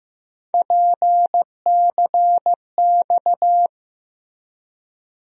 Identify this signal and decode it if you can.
Morse code, heard as a keyed tone, about 15 wpm, 700 Hz. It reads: PCX
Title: USA Call Sign Postfixes - 3 Letters 15wpm